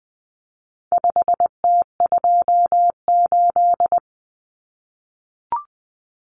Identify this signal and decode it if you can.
Morse code, heard as a keyed tone, about 20 wpm, 700 Hz. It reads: 5T28
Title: Common Cut Numbers - 4-Digit 20wpm